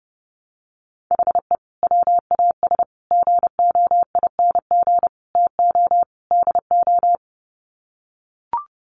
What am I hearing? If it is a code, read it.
Morse code, heard as a keyed tone, about 28 wpm, 700 Hz. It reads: HE WAS GOING TO DO
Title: Sets of 5 Words Encore2 30wpm